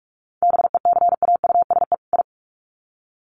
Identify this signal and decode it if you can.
Morse code, heard as a keyed tone, about 44 wpm, 700 Hz. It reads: BECAUSE I
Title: Sets of 2 Words 45wpm